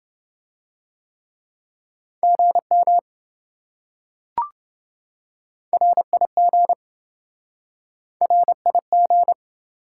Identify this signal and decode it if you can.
Morse code, heard as a keyed tone, about 27 wpm, 700 Hz. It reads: GM RIG RIG